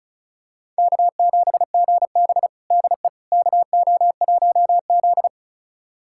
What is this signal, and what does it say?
Morse code, heard as a keyed tone, about 34 wpm, 700 Hz. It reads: K7GB DE KO1Z
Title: USA Callsign DE Callsign - 4 Characters - 35wpm